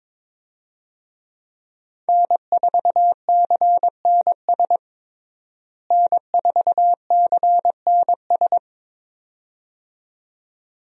Morse code, heard as a keyed tone, about 22 wpm, 700 Hz. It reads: N4CNS N4CNS
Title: USA Call Signs - 5 Characters - Repeated 2x - 22wpm